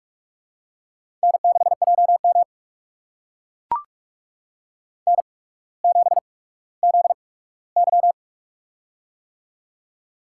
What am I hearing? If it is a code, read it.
Morse code, heard as a keyed tone, about 44 wpm, spaced as for 19 wpm, 700 Hz. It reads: N6JM N7ZY
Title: USA Call Signs - 4 Characters - 8x Character Spacing - 45wpm